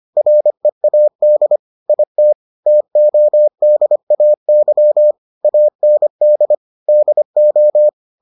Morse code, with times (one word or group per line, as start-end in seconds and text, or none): 0.2-1.6 READ
1.9-2.3 IT
2.7-5.1 TODAY
5.4-6.5 AND
6.9-7.9 DO